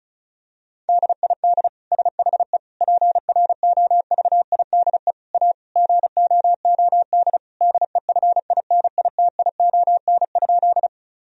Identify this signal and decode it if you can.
Morse code, heard as a keyed tone, about 36 wpm, 700 Hz. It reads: DID SHE PROVIDE A GOOD DEFINITION?
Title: Sentences from Top 1000 Words - Review Part 4 - 35wpm